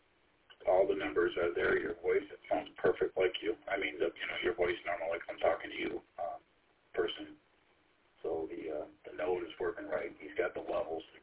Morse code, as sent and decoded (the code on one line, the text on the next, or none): none